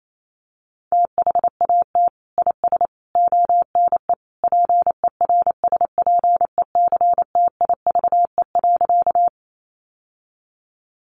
Morse code, heard as a keyed tone, about 28 wpm, 700 Hz. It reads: THAT IS ONE PERSPECTIVE.
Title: Sentences from Top 1200 Words 28wpm